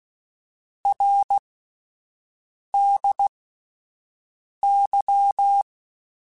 Morse code, heard as a keyed tone, about 16 wpm, spaced as for 5 wpm, 800 Hz. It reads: RDY
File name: Abreviations / Abreviations mp3s / Abreviations@5